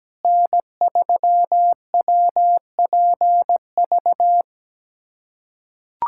Morse code, heard as a keyed tone, about 17 wpm, 700 Hz. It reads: N3WPV
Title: USA Callsign DE Callsign - 5 Characters - 17wpm